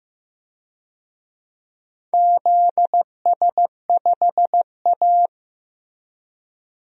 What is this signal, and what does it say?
Morse code, heard as a keyed tone, about 15 wpm, 700 Hz. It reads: ZS5A
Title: DXCC-100 Call Signs - 4 Characters - Repeated 2x - 15wpm